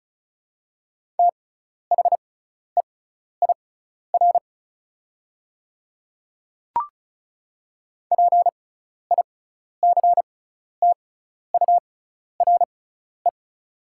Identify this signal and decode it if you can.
Morse code, heard as a keyed tone, about 36 wpm, spaced as for 12 wpm, 690 Hz. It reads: THEIR PICTURE
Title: 6x Character Spacing - Top 200 Words - 35wpm